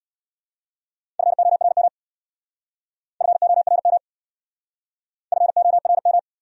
Morse code, heard as a keyed tone, about 69 wpm, 700 Hz. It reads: FORM FORM FORM